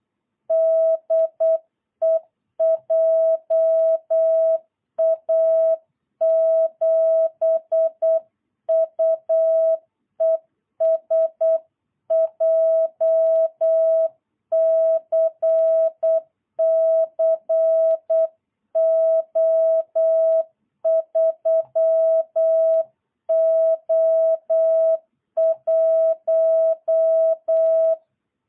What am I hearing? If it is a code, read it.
Morse code, heard as a keyed tone, about 8 wpm, 660 Hz. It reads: DE JA7UES JCC O3O1
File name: jh8cbh / 2022cw